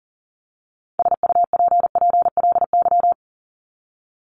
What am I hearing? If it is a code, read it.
Morse code, heard as a keyed tone, about 40 wpm, 700 Hz. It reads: SUPPLY
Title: Top 501-750 Words 40wpm